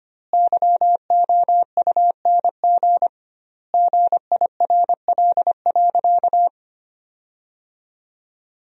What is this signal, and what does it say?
Morse code, heard as a keyed tone, about 25 wpm, 700 Hz. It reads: YOUNG GIRL.
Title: 2x Word-Spacing - Sentences from Top 400 Words - 25wpm